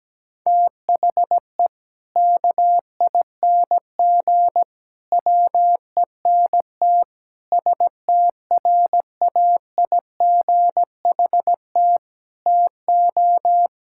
Morse code, heard as a keyed tone, about 17 wpm, 700 Hz. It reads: THE KING WENT STRAIGHT TO
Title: Sentences from Top 1000 Words - Review Part 6 - 17wpm